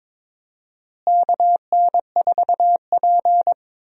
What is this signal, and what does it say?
Morse code, heard as a keyed tone, about 22 wpm, 700 Hz. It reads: KN4P
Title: USA Call Signs - 4 Characters - Repeated 3x - 22wpm